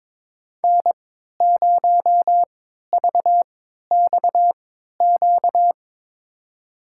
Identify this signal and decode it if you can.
Morse code, heard as a keyed tone, about 22 wpm, spaced as for 13 wpm, 700 Hz. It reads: N0VXQ
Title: USA Call Signs - 5 Characters - 3x Character Spacing - 22wpm